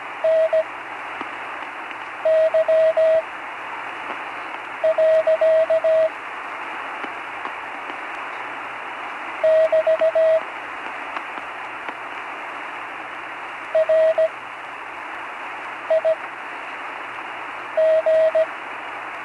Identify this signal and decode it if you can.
Morse code, heard as a keyed tone, about 17 wpm, spaced as for 5 wpm, 640 Hz, with radio noise. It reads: NY. <BT> RIG
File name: CW Learning Programs / Learning Morse Code (from Tune in the World) / 43 on the air